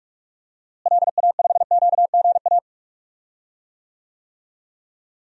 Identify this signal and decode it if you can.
Morse code, heard as a keyed tone, about 46 wpm, 700 Hz. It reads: LA5QGA
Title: DXCC-100 Call Signs - 6 Characters - Repeated 2x - 45wpm